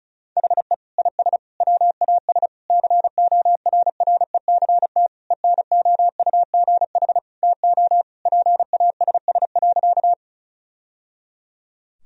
Morse code, heard as a keyed tone, about 36 wpm, 700 Hz. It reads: HE IS WAS CORRECT ENOUGH TO PASS.
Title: Sentences from Top 1000 Words - Review Part 2 - 35wpm